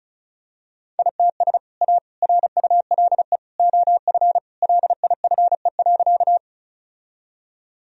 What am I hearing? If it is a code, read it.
Morse code, heard as a keyed tone, about 36 wpm, 700 Hz. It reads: ITS A RULE OF LIFE.
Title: Sentences from Top 1000 Words - Review Part 2 - 35wpm